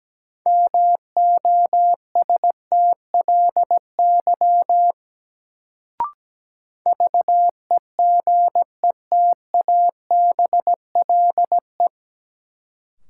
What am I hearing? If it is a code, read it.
Morse code, heard as a keyed tone, about 17 wpm, 700 Hz. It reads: MOSTLY VEGETABLE
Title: Sentences from Top 1300 Words - New Word Drill - 17wpm